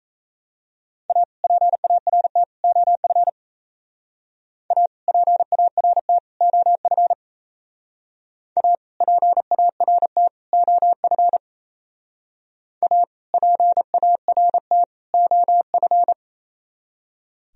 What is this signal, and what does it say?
Morse code, heard as a keyed tone, about 34 wpm, 700 Hz. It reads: A PART OF A PART OF A PART OF A PART OF
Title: Speed-Racing - Sets of 3 Words 28wpm